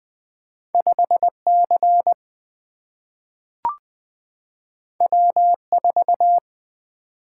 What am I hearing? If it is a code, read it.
Morse code, heard as a keyed tone, about 20 wpm, 700 Hz. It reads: 5 C W 4
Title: Single-Letter Single-Number 20wpm